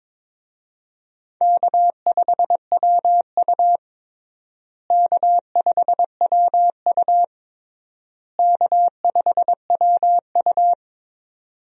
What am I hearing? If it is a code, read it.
Morse code, heard as a keyed tone, about 22 wpm, 700 Hz. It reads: K5WU K5WU K5WU
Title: USA Call Signs - 4 Characters - Repeated 3x - 22wpm